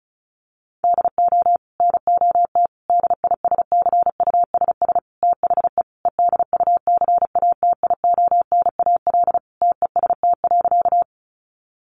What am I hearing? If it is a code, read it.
Morse code, heard as a keyed tone, about 34 wpm, 700 Hz. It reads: DO NOT DISCUSS THE EDUCATIONAL TEST.
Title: Sentences from Top 1200 Words 35wpm